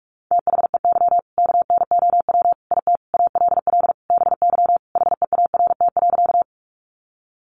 Morse code, heard as a keyed tone, about 44 wpm, 700 Hz. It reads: THEY KNOW IT ALL BY HEART.
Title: Sentences from Top 1200 Words 45wpm